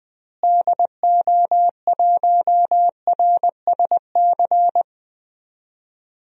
Morse code, heard as a keyed tone, about 20 wpm, 700 Hz. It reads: DO1RSC
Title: DXCC-100 Call Signs - 6 Characters - Repeated 2x - 20wpm